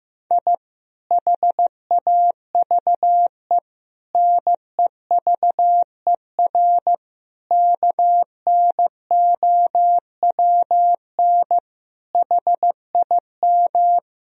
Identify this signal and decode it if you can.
Morse code, heard as a keyed tone, about 15 wpm, 700 Hz. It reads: I HAVE NEVER KNOWN HIM
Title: Sentences from Top 1000 Words - Review Part 3 - 15wpm